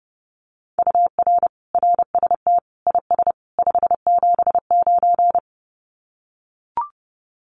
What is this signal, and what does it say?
Morse code, heard as a keyed tone, about 30 wpm, 700 Hz. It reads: UR RST IS 579
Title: Common QSO Phrases 30wpm